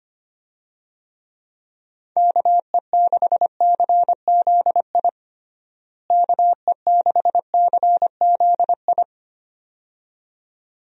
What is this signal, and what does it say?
Morse code, heard as a keyed tone, about 25 wpm, 700 Hz. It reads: KE6CZI KE6CZI
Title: USA Call Signs - 6 Characters - Repeated 2x - 25wpm